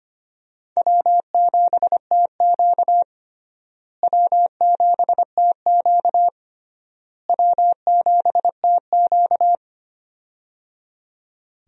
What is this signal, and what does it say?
Morse code, heard as a keyed tone, about 25 wpm, 700 Hz. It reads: W7TQ W7TQ W7TQ
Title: USA Call Signs - 4 Characters - Repeated 3x - 25wpm